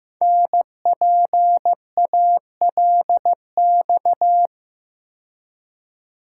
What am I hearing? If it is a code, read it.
Morse code, heard as a keyed tone, about 15 wpm, 700 Hz. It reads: NPALX
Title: Code Groups - 5 Letter:Numbers - 15wpm